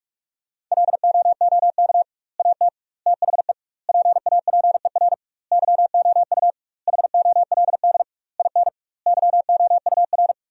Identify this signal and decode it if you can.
Morse code, heard as a keyed tone, about 46 wpm, 700 Hz. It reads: LOOK AT THE PAPER YOU HOLD IN YOUR
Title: 2x Word-Spacing - Sentences from Top 400 Words - 45wpm